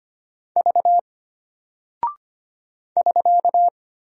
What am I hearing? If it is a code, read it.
Morse code, heard as a keyed tone, about 25 wpm, 700 Hz. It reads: V <SK>
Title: Single Character (Letter:Number:Punctuation:Prosign) Alt 25wpm